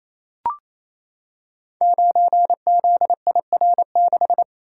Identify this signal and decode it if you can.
Morse code, heard as a keyed tone, about 28 wpm, 700 Hz. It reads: 9ZIR6
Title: Code Groups - 5 Letter:Numbers - 28wpm